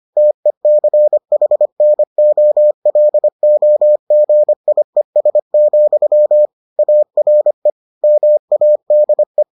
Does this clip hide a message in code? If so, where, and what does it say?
Yes: Morse code from 0.2 to 9.4 s, heard as a keyed tone, about 25 wpm, 600 Hz. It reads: TECHNOLOGIES, ARE MADE